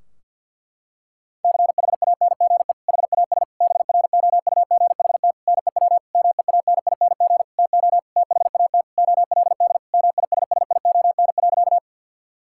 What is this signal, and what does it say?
Morse code, heard as a keyed tone, about 52 wpm, 700 Hz. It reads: CHANGE HAS BROUGHT NEW MEANING TO THAT OLD MISSION.
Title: Sentences from Top 1600 Words 50wpm